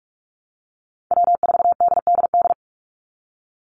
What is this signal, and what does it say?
Morse code, heard as a keyed tone, about 44 wpm, 700 Hz. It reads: W4DDD
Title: Call Signs Repeated 3x - 45wpm